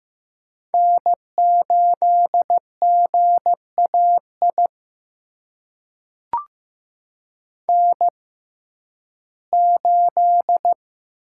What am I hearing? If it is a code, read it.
Morse code, heard as a keyed tone, about 15 wpm, spaced as for 11 wpm, 700 Hz. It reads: N8GAI N8
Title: USA Call Signs - 5 Characters - 6x Character Spacing - 15wpm